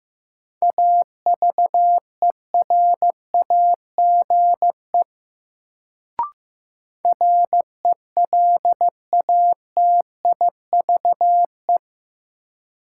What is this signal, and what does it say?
Morse code, heard as a keyed tone, about 15 wpm, 700 Hz. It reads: AVERAGE RELATIVE
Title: Sentences from Top 1400 Words - New Word Drill Alt - 15wpm